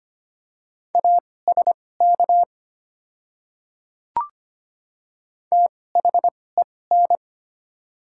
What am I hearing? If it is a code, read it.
Morse code, heard as a keyed tone, about 25 wpm, spaced as for 18 wpm, 700 Hz. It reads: ASK THEN
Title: 2x Character Spacing - Top 200 Words - 25wpm